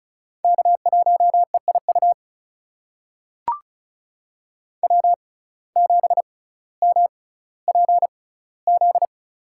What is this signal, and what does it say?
Morse code, heard as a keyed tone, about 36 wpm, spaced as for 18 wpm, 700 Hz. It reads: K1EIU W7MPZ
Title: USA Call Signs - 5 Characters - 6x Character Spacing - 35wpm